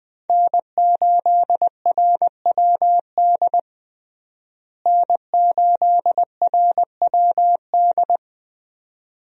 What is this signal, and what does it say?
Morse code, heard as a keyed tone, about 21 wpm, 700 Hz. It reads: N8RWD N8RWD
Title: USA Call Signs - 5 Characters - Repeated 2x - 20wpm